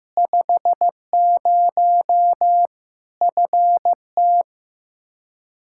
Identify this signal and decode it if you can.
Morse code, heard as a keyed tone, about 15 wpm, 700 Hz. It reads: 50 FT
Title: Practice QSO - Exchange 2 - Info - 15wpm